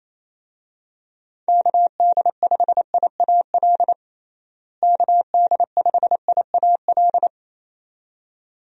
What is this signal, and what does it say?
Morse code, heard as a keyed tone, about 28 wpm, 700 Hz. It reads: KD5IAL KD5IAL
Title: USA Call Signs - 6 Characters - Repeated 2x - 28wpm